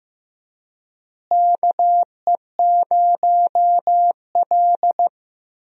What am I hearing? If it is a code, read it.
Morse code, heard as a keyed tone, about 15 wpm, 700 Hz. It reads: KE0L
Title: USA Call Signs - 4 Characters - Repeated 3x - 15wpm